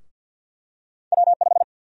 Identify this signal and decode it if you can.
Morse code, heard as a keyed tone, about 52 wpm, 700 Hz. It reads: W5
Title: USA Call Sign Prefixes - 50wpm